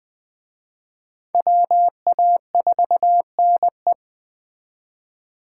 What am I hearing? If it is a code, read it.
Morse code, heard as a keyed tone, about 20 wpm, 700 Hz. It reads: WA4NE
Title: Speed Racing - USA Call Signs 17wpm